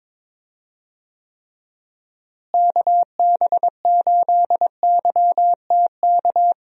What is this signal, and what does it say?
Morse code, heard as a keyed tone, about 22 wpm, 700 Hz. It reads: KB8YTK
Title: USA Call Signs - 6 Characters - Repeated 2x - 22wpm